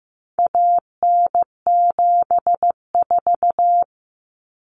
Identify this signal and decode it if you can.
Morse code, heard as a keyed tone, about 15 wpm, 700 Hz. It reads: AN74
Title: Common Cut Numbers - 4-Digit 15wpm